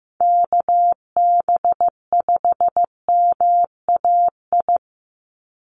Morse code, heard as a keyed tone, about 15 wpm, 700 Hz. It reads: KB5MAI
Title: USA Call Signs 15wpm